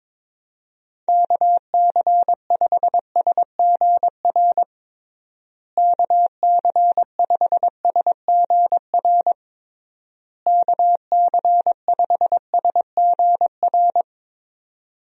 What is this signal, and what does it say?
Morse code, heard as a keyed tone, about 22 wpm, 700 Hz. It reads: KC5SGR KC5SGR KC5SGR
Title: USA Call Signs - 6 Characters - Repeated 3x - 22wpm